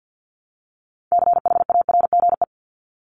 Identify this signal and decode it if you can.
Morse code, heard as a keyed tone, about 50 wpm, 700 Hz. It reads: CHARGE
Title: Top 751-1000 Words 50wpm